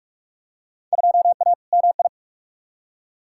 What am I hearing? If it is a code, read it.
Morse code, heard as a keyed tone, about 44 wpm, 700 Hz. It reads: 2A MI
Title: Field Day Exchanges - 45wpm